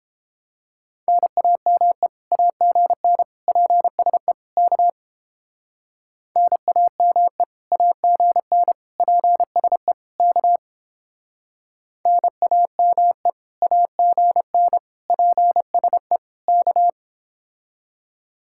Morse code, heard as a keyed tone, about 31 wpm, 700 Hz. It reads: NAME AGN PSE K NAME AGN PSE K NAME AGN PSE K